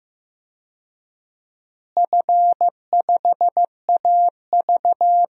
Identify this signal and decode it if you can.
Morse code, heard as a keyed tone, about 15 wpm, 700 Hz. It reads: F5AV
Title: DXCC-100 Call Signs - 4 Characters - Repeated 2x - 15wpm